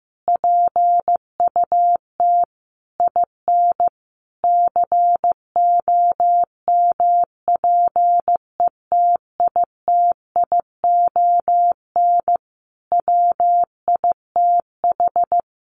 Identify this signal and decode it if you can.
Morse code, heard as a keyed tone, about 15 wpm, 700 Hz. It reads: PUT IN COMPETITION WITH
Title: Sentences from Top 1000 Words 15wpm